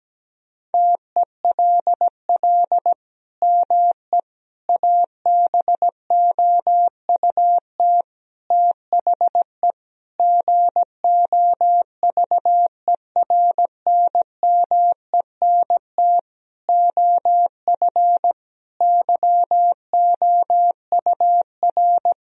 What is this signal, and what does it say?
Morse code, heard as a keyed tone, about 17 wpm, 700 Hz. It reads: TELL ME ABOUT THE GOVERNMENT OF YOUR
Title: Sentences from Top 1000 Words - Review Part 3 - 17wpm